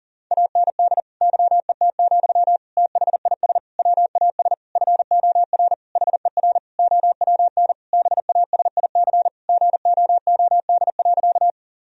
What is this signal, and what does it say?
Morse code, heard as a keyed tone, about 37 wpm, 700 Hz. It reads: AND YET, THIS WAS FOR HER OWN BASIC GOOD.
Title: Sentences from Top 1000 Words - Review Part 3 - 40wpm